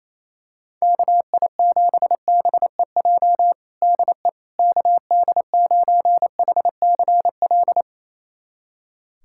Morse code, heard as a keyed tone, about 28 wpm, 700 Hz. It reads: KI7BEJ DE KD9HCL
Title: USA Callsign DE Callsign - 6 Characters - 28wpm